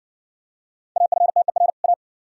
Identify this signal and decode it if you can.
Morse code, heard as a keyed tone, about 57 wpm, 700 Hz. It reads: AFTER A